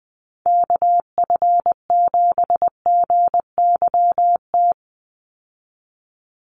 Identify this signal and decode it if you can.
Morse code, heard as a keyed tone, about 20 wpm, 700 Hz. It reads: KF7GYT
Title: USA Call Signs 20wpm